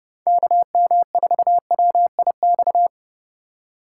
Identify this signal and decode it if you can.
Morse code, heard as a keyed tone, about 30 wpm, 700 Hz. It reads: KM4WIX
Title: USA Call Signs - 6 Characters - 3x Character Spacing - 30wpm